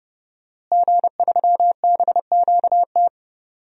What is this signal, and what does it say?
Morse code, heard as a keyed tone, about 30 wpm, 700 Hz. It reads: G3BQT
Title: DXCC-100 Call Signs - 5 Characters - Repeated 3x - 30wpm